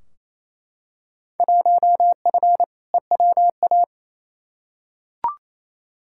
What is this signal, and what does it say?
Morse code, heard as a keyed tone, about 28 wpm, 700 Hz. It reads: 1F EWA
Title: Field Day Exchanges Encore - 28wpm